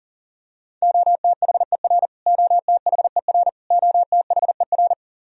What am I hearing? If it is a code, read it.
Morse code, heard as a keyed tone, about 37 wpm, 700 Hz. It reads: OTHER OTHER OTHER
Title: Top 100 Words Repeated 40wpm